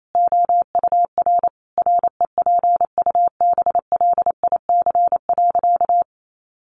Morse code, heard as a keyed tone, about 28 wpm, 700 Hz. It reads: OUR REPUBLIC.
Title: Sentences from Top 1500 Words 28wpm